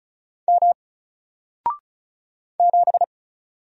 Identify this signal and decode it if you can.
Morse code, heard as a keyed tone, about 36 wpm, 700 Hz. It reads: M 7
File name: Single-Letter-Number-35wpm